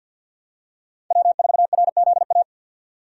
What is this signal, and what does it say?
Morse code, heard as a keyed tone, about 50 wpm, 700 Hz. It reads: W4RZA
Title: USA Call Signs - 5 Characters - 50wpm